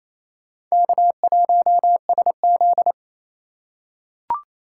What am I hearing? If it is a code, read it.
Morse code, heard as a keyed tone, about 28 wpm, 700 Hz. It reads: K1SZ
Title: USA Call Signs - 4 Characters - 2x Character Spacing - 28wpm